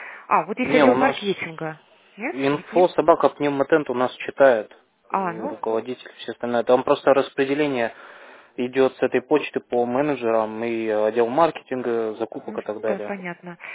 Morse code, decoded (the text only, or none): none